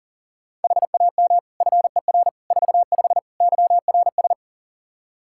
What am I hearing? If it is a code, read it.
Morse code, heard as a keyed tone, about 40 wpm, 700 Hz. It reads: HAM FER 45 YRS
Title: Practice QSO - Exchange 3 - More Info - 40wpm